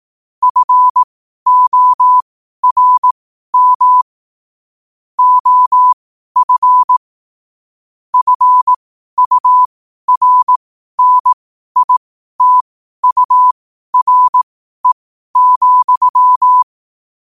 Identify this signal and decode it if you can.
Morse code, heard as a keyed tone, about 18 wpm, spaced as for 12 wpm, 1000 Hz. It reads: FORM OF FURNITURE,
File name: qotdfiles / 12wpm